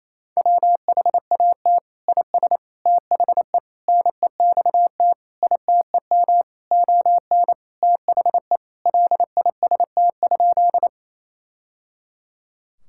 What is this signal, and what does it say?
Morse code, heard as a keyed tone, about 27 wpm, 700 Hz. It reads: WHAT IS THE NEXT ITEM ON THE LIST?
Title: Sentences from Top 1000 Words - Review Part 6 - 28wpm